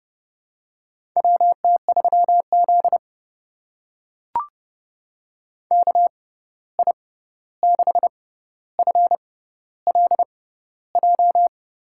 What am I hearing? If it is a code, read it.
Morse code, heard as a keyed tone, about 30 wpm, spaced as for 14 wpm, 700 Hz. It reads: WT3Z KI6FLJ
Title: USA Call Signs - 6x Character Spacing - 30wpm